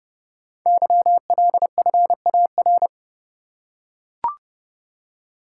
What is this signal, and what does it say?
Morse code, heard as a keyed tone, about 30 wpm, 700 Hz. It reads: YLFAR
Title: Code Groups - 5 Letters - 30wpm